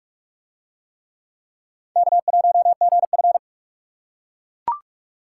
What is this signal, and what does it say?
Morse code, heard as a keyed tone, about 38 wpm, 700 Hz. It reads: K1GF